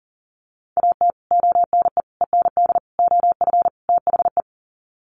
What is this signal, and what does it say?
Morse code, heard as a keyed tone, about 40 wpm, 700 Hz. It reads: AT ONE END OF THE
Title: Sets of 5 Words Encore 40wpm